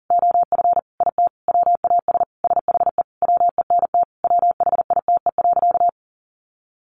Morse code, heard as a keyed tone, about 40 wpm, 700 Hz. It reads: OF IT WAS SHE WENT WHITE.
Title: Sentences from Top 1200 Words 40wpm